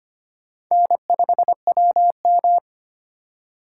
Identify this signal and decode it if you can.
Morse code, heard as a keyed tone, about 31 wpm, 700 Hz. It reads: N5WM
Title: Speed Racing - USA 1x2 Call Signs 25wpm